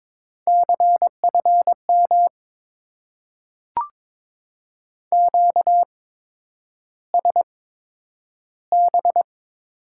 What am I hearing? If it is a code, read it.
Morse code, heard as a keyed tone, about 22 wpm, spaced as for 9 wpm, 700 Hz. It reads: CFM QSB
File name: Top 90 QSO Elements - 8x Character Spacing - 22wpm